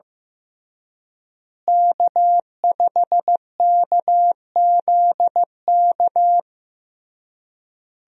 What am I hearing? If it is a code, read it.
Morse code, heard as a keyed tone, about 15 wpm, 700 Hz. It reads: K5KZK
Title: USA Call Signs - 5 Characters - Repeated 3x - 15wpm